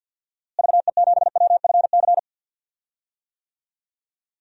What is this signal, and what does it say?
Morse code, heard as a keyed tone, about 47 wpm, 700 Hz. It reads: VE7WFC
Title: DXCC-100 Call Signs - 6 Characters - Repeated 3x - 50wpm